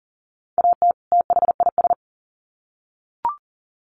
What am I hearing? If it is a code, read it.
Morse code, heard as a keyed tone, about 40 wpm, 700 Hz. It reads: AT THIS
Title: Sets of 2 Words 40wpm